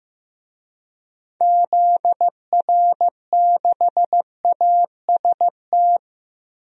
Morse code, heard as a keyed tone, about 15 wpm, 700 Hz. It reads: ZR6AST
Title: DXCC-100 Call Signs - 6 Characters - 15wpm